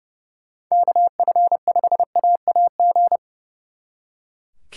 Morse code, heard as a keyed tone, about 35 wpm, 700 Hz. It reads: KF5AAG
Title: Speed Racing - USA 2x3 Call Signs 30wpm